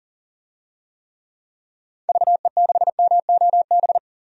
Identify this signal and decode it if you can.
Morse code, heard as a keyed tone, about 40 wpm, 700 Hz. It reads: VE6MOB
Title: DXCC-100 Call Signs - 6 Characters - 40wpm